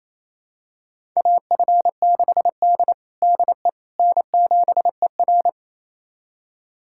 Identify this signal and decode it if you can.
Morse code, heard as a keyed tone, about 28 wpm, 700 Hz. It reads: AF6D DE N7ER
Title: USA Callsign DE Callsign - 4 Characters - 28wpm